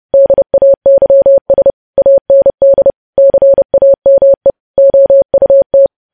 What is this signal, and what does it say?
Morse code, heard as a keyed tone, about 30 wpm, 550 Hz. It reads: DAYS AND CAME OUT
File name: podcast / News headlines-1x-30x30